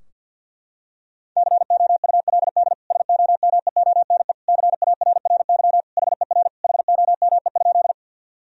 Morse code, heard as a keyed tone, about 49 wpm, 700 Hz. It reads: COULD SOMEONE CARRY HER HOME?
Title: Sentences from Top 1000 Words - Review Part 4 - 50wpm